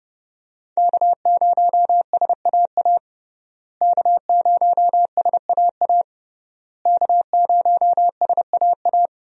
Morse code, heard as a keyed tone, about 32 wpm, 700 Hz. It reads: K0SAA K0SAA K0SAA
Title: USA Call Signs - 5 Characters - Repeated 3x - 30wpm